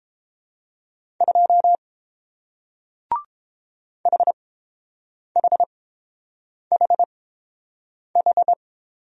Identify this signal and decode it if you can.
Morse code, heard as a keyed tone, about 30 wpm, spaced as for 7 wpm, 700 Hz. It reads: 2 HHHH